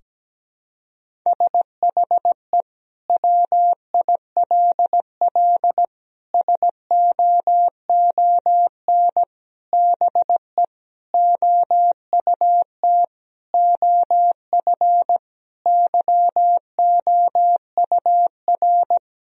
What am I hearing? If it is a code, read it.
Morse code, heard as a keyed tone, about 17 wpm, 700 Hz. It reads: SHE WILL SOON BE OUT OF YOUR
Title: Sentences from Top 1000 Words - Review Part 2 - 17wpm